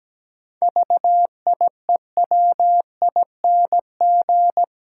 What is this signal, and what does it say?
Morse code, heard as a keyed tone, about 17 wpm, 700 Hz. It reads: VIEWING
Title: Pangram Challenge v2 - 17wpm